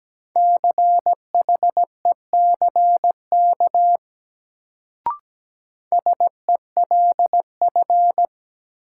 Morse code, heard as a keyed tone, about 17 wpm, 700 Hz. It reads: CHECK SELF
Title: Sentences from Top 1400 Words - New Word Drill Alt - 17wpm